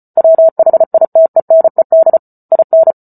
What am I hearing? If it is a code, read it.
Morse code, heard as a keyed tone, about 34 wpm, 650 Hz. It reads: WHITENED IN